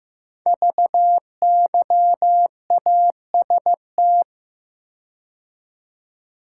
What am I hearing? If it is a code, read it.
Morse code, heard as a keyed tone, about 15 wpm, 700 Hz. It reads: VYAST
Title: Code Groups - 5 Letters - 15wpm